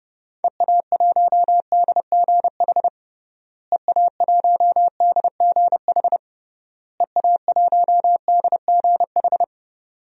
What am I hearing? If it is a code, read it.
Morse code, heard as a keyed tone, about 32 wpm, 700 Hz. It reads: EA1DGH EA1DGH EA1DGH
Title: DXCC-100 Call Signs - 6 Characters - Repeated 3x - 30wpm